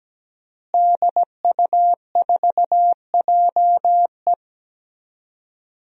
Morse code, heard as a keyed tone, about 17 wpm, 700 Hz. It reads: DU4JE
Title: DXCC-100 Call Signs - 5 Characters - Repeated 3x - 17wpm